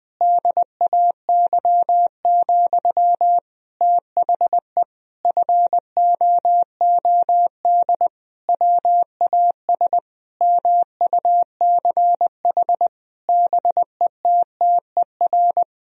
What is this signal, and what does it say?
Morse code, heard as a keyed tone, about 20 wpm, 700 Hz. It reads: DAY, THE FOOD WAS MUCH BETTER
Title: Sentences from Top 1000 Words - Review Part 2 - 20wpm